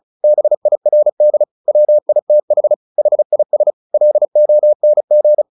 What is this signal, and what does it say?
Morse code, heard as a keyed tone, about 34 wpm, 600 Hz. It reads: BIRD WITH HIS LONG
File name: the old man and the sea / the old man and the sea 1 35wpm